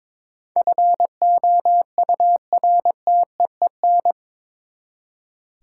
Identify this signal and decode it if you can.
Morse code, heard as a keyed tone, about 22 wpm, 700 Hz. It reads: FOURTEEN
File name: practice / 22wpm / Eight Letter Words - 22wpm